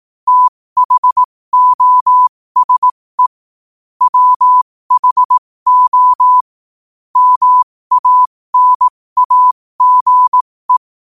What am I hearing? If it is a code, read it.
Morse code, heard as a keyed tone, about 18 wpm, spaced as for 15 wpm, 1000 Hz. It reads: THOSE WHO MANAGE
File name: qotdfiles / 15wpm